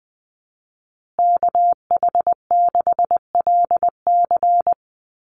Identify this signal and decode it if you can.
Morse code, heard as a keyed tone, about 20 wpm, 700 Hz. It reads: KH6LC
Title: Call Signs Repeated 3x - 20wpm